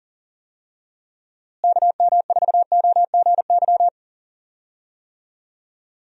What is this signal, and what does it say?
Morse code, heard as a keyed tone, about 38 wpm, 700 Hz. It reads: KM4OGY
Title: USA Call Signs - 6 Characters - Repeated 3x - 40wpm